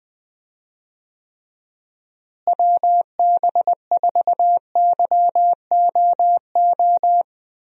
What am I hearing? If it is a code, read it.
Morse code, heard as a keyed tone, about 20 wpm, 700 Hz. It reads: WB4YOO
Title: USA Call Signs - 6 Characters - Repeated 3x - 20wpm